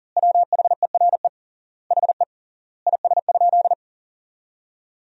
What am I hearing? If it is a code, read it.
Morse code, heard as a keyed tone, about 40 wpm, 700 Hz. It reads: WHERE HE IS?
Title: 3x Word-Spacing - Sentences from Top 300 Words - 40wpm